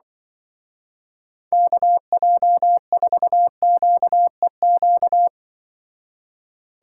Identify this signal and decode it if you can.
Morse code, heard as a keyed tone, about 24 wpm, 700 Hz. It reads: KJ4QEQ
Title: Speed Racing - USA 2x3 Call Signs 20wpm